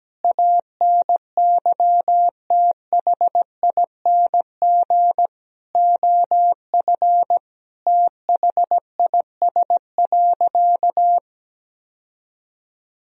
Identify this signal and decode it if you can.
Morse code, heard as a keyed tone, about 17 wpm, 700 Hz. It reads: ANYTHING OF THIS.
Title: Sentences from Top 1000 Words - Review Part 3 - 17wpm